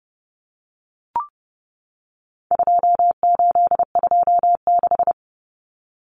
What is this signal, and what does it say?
Morse code, heard as a keyed tone, about 29 wpm, 700 Hz. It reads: 2826
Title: Numbers - 4-Digit 30wpm